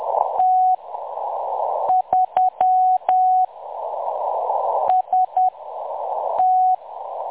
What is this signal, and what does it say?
Morse code, heard as a keyed tone, about 10 wpm, spaced as for 6 wpm, 740 Hz. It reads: T3 ST
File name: M23-ST3- -0858utc-10184khz